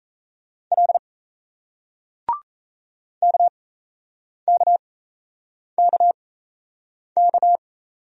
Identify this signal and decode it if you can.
Morse code, heard as a keyed tone, about 39 wpm, spaced as for 8 wpm, 700 Hz. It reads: L KKKK